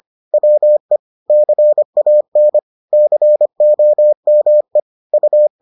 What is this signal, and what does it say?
Morse code, heard as a keyed tone, about 25 wpm, 600 Hz. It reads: WE CAN COME U